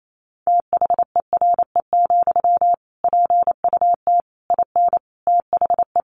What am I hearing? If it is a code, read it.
Morse code, heard as a keyed tone, about 28 wpm, 700 Hz. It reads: THERE, PUT IN THE
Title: Sentences from Top 1200 Words 28wpm